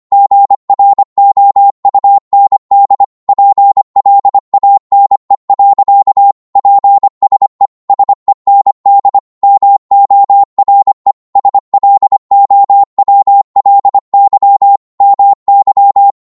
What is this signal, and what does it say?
Morse code, heard as a keyed tone, about 25 wpm, 800 Hz. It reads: GROUND PLANE. PSE SEND MORE SLOWLY MY
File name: Morse / qso-25wpm